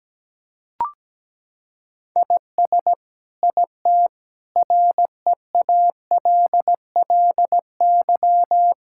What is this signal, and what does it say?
Morse code, heard as a keyed tone, about 17 wpm, 700 Hz. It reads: IS IT REALLY
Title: Sentences from Top 1000 Words - Review Part 6 - 17wpm